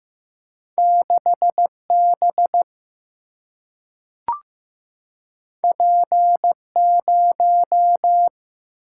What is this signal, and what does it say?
Morse code, heard as a keyed tone, about 15 wpm, 700 Hz. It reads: 6B P0
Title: Single-Letter Single-Number 15wpm